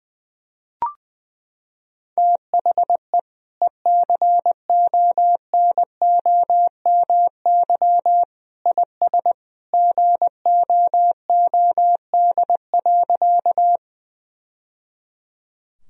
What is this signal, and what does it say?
Morse code, heard as a keyed tone, about 20 wpm, 700 Hz. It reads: THE ECONOMY IS GOOD.
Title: Sentences from Top 1000 Words - Review Part 3 - 20wpm